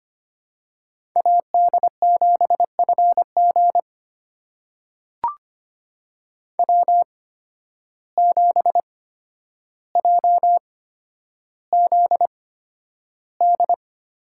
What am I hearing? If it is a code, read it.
Morse code, heard as a keyed tone, about 25 wpm, spaced as for 11 wpm, 700 Hz. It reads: AD7FG W7JZD
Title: USA Call Signs - 5 Characters - 8x Character Spacing - 25wpm